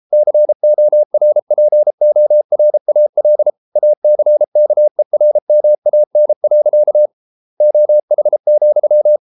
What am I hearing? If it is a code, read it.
Morse code, heard as a keyed tone, about 33 wpm, 600 Hz. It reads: CORPORAL ACKERMAN. OH,